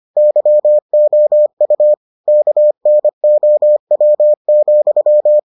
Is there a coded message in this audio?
Yes: Morse code from 0.2 to 5.4 s, heard as a keyed tone, about 25 wpm, 600 Hz. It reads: YOU KNOW,